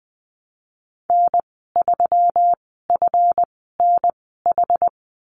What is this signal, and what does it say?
Morse code, heard as a keyed tone, about 20 wpm, spaced as for 14 wpm, 700 Hz. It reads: N3FNH
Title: 140 Lesson 23g - Letter f - Call Signs 20wpm